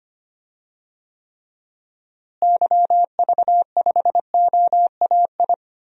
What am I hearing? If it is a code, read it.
Morse code, heard as a keyed tone, about 25 wpm, 700 Hz. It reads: YV5OAI
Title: DXCC-100 Call Signs - 6 Characters - Repeated 3x - 25wpm